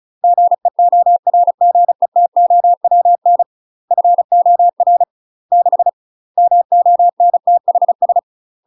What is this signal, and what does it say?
Morse code, heard as a keyed tone, about 34 wpm, 700 Hz. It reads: GEORGETOWN FOR 6 MONTHS
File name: UK MASTER QSOs TRAINING/35WPM